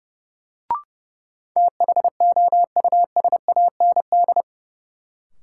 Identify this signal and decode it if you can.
Morse code, heard as a keyed tone, about 30 wpm, 700 Hz. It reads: THOUSAND
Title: Sentences from Top 600 Words - New Word Drill - 30wpm